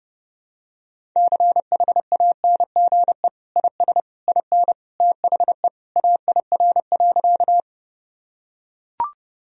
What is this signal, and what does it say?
Morse code, heard as a keyed tone, about 32 wpm, 700 Hz. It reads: CHANGE IS IN THE AIR.
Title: Sentences from Top 1000 Words - Review Part 1 - 30wpm